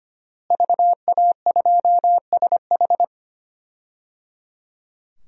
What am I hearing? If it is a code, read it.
Morse code, heard as a keyed tone, about 25 wpm, 700 Hz. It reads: VA2SH
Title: DXCC-100 Call Signs - 5 Characters - Repeated 3x - 25wpm